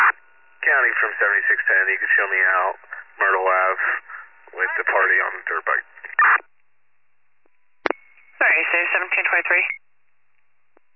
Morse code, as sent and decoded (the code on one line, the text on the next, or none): none